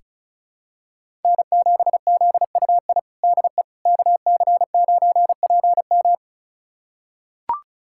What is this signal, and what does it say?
Morse code, heard as a keyed tone, about 36 wpm, 700 Hz. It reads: N7ZUI DE KC9PM
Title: USA Callsign DE Callsign - 5 Characters - 35wpm